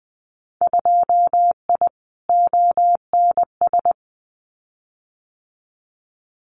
Morse code, heard as a keyed tone, about 20 wpm, 700 Hz. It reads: 2I ONS
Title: Winter Field Day Exchanges Encore 20wpm